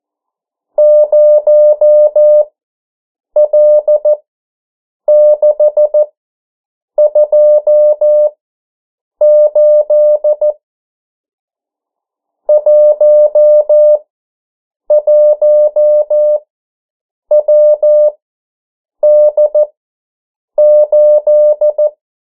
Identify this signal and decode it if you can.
Morse code, heard as a keyed tone, about 14 wpm, spaced as for 7 wpm, 600 Hz. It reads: 0L628 11WD8
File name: morse mp3 / 09 urok 8